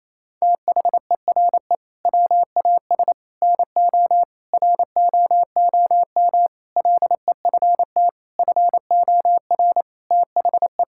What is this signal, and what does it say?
Morse code, heard as a keyed tone, about 28 wpm, 700 Hz. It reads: THERE WAS NO ROOM LEFT FOR THE
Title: Sentences from Top 1600 Words 28wpm